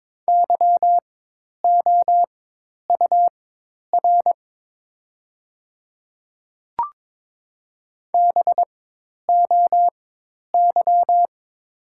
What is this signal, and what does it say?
Morse code, heard as a keyed tone, about 22 wpm, spaced as for 10 wpm, 690 Hz. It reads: YOUR BOY
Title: 4x Character Spacing - Top 300 Words - 22wpm